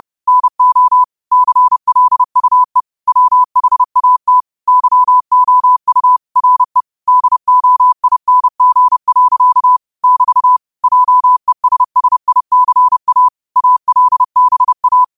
Morse code, heard as a keyed tone, about 30 wpm, 1000 Hz. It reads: NO CLUE WHAT YOU RE DOING. <BT> JESSICA ALBA